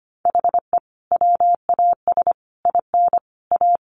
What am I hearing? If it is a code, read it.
Morse code, heard as a keyed tone, about 25 wpm, 700 Hz. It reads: HE WAS IN A
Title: Sentences from Top 1200 Words 25wpm